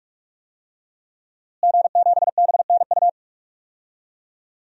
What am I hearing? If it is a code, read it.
Morse code, heard as a keyed tone, about 46 wpm, 700 Hz. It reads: G7BNU
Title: DXCC-100 Call Signs - 5 Characters - Repeated 3x - 45wpm